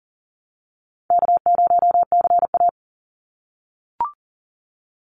Morse code, heard as a keyed tone, about 40 wpm, 700 Hz. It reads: K0CA
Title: Easy Call Signs - 40wpm